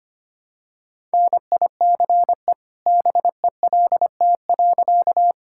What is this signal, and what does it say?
Morse code, heard as a keyed tone, about 25 wpm, 700 Hz. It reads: NICE BELT.
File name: practice / 25wpm / All-Age Appropriate Jokes - 25wpm